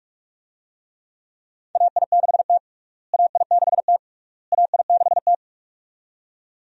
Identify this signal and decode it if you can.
Morse code, heard as a keyed tone, about 46 wpm, 700 Hz. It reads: AI6T AI6T AI6T
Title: USA Call Signs - 4 Characters - Repeated 3x - 45wpm